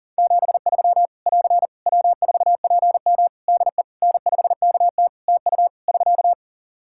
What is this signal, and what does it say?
Morse code, heard as a keyed tone, about 40 wpm, 700 Hz. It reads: 73 <AR> W4PM DE N5KT TU <SK>
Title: K1USN SST QSO Practice 40wpm